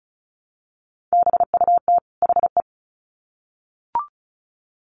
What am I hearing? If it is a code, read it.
Morse code, heard as a keyed tone, about 34 wpm, 700 Hz. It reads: BUT HE
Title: Sets of 2 Words 35wpm